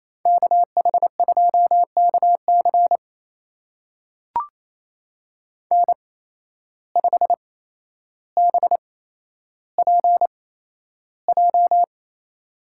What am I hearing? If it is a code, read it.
Morse code, heard as a keyed tone, about 28 wpm, spaced as for 12 wpm, 700 Hz. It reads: KH2KC N5BPJ
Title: USA Call Signs - 5 Characters - 8x Character Spacing - 28wpm